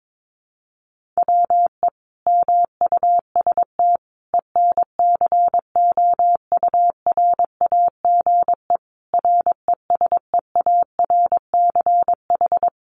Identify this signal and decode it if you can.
Morse code, heard as a keyed tone, about 22 wpm, 700 Hz. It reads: WE MUST ENCOURAGE RESEARCH
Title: Sentences from Top 1400 Words 22wpm